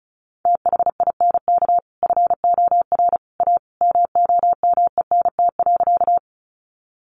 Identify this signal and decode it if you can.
Morse code, heard as a keyed tone, about 34 wpm, 700 Hz. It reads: THINK FOR A MOMENT.
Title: Sentences from Top 1000 Words 35wpm